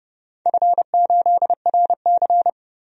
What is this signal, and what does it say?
Morse code, heard as a keyed tone, about 31 wpm, 700 Hz. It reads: F8RC
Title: DXCC-100 Call Signs - 4 Characters - 30wpm